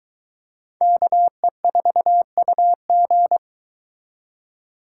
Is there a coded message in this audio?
Yes: Morse code from 0.8 to 3.4 s, heard as a keyed tone, about 22 wpm, 700 Hz. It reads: KE4UG